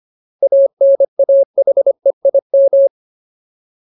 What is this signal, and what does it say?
Morse code, heard as a keyed tone, about 25 wpm, 550 Hz. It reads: ANAHEIM
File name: Ultimate Word List / 25WPM Ultimate